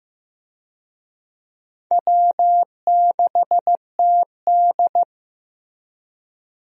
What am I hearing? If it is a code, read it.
Morse code, heard as a keyed tone, about 15 wpm, 700 Hz. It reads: W6TD
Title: USA Call Signs - 4 Characters - Repeated 2x - 15wpm